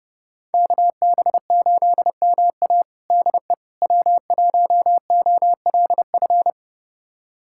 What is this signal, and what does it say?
Morse code, heard as a keyed tone, about 30 wpm, 700 Hz. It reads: KB8MA DE W1OLF
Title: USA Callsign DE Callsign - 5 Characters - 30wpm